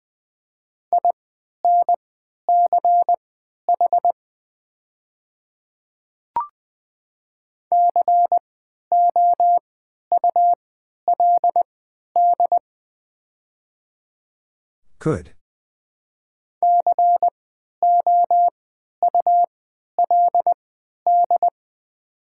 Morse code, heard as a keyed tone, about 20 wpm, spaced as for 11 wpm, 700 Hz. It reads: INCH COULD COULD
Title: 3x Character Spacing - Top 500 Words - 20wpm